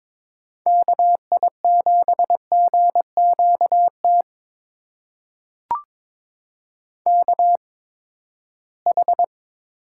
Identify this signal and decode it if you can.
Morse code, heard as a keyed tone, about 22 wpm, spaced as for 15 wpm, 700 Hz. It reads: KI7GQT KH
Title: USA Call Signs - 6 Characters - 8x Character Spacing - 22wpm